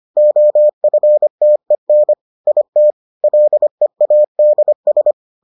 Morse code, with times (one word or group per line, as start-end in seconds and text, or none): 0.2-2.1 OFTEN
2.5-2.9 IT
3.2-5.1 LEADS